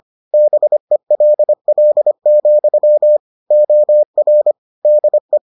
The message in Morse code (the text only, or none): BELL, OR DE